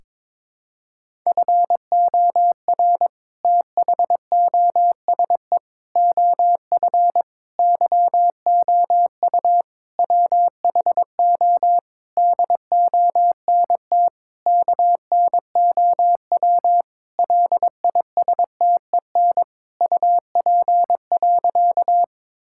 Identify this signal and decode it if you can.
Morse code, heard as a keyed tone, about 22 wpm, 700 Hz. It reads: FOR THOSE OF YOU WHO DONT KNOW LISTEN UP.
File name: Sentences with 7-Word Phrases - 22wpm